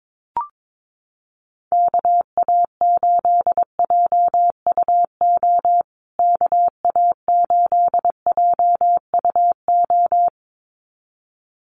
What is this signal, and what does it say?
Morse code, heard as a keyed tone, about 22 wpm, 700 Hz. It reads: KA8JUO KA8JUO
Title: USA Call Signs Repeated 2x 22wpm